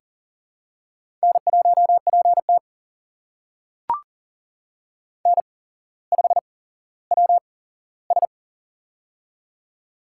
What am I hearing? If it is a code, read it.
Morse code, heard as a keyed tone, about 39 wpm, spaced as for 17 wpm, 700 Hz. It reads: N1PT N5WS
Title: USA Call Signs - 4 Characters - 8x Character Spacing - 40wpm